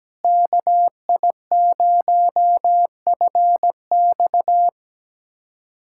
Morse code, heard as a keyed tone, about 17 wpm, 700 Hz. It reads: KI0FX
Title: USA Callsign DE Callsign 17wpm